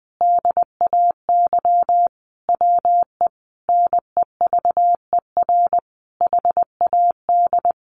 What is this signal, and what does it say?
Morse code, heard as a keyed tone, about 20 wpm, 700 Hz. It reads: DAY WE NEVER HAD
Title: Sentences from Top 1100 Words 20wpm